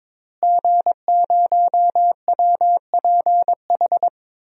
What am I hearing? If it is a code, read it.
Morse code, heard as a keyed tone, about 22 wpm, 700 Hz. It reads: G0WPH
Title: DXCC-100 Call Signs - 5 Characters - Repeated 3x - 22wpm